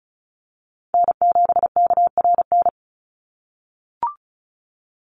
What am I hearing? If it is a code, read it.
Morse code, heard as a keyed tone, about 34 wpm, 700 Hz. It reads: N7KRN
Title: NA SOTA Chaser Callsigns 35wpm